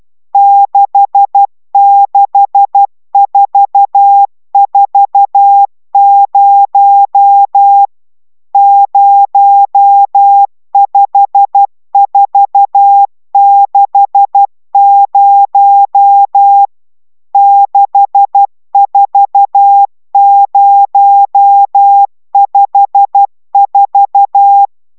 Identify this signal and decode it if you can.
Morse code, heard as a keyed tone, about 12 wpm, 800 Hz. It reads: 66440 05460 64054